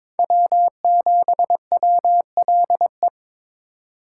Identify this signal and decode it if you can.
Morse code, heard as a keyed tone, about 26 wpm, 700 Hz. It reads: W7WLE
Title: Speed Racing - USA 1x3 Call Signs 22wpm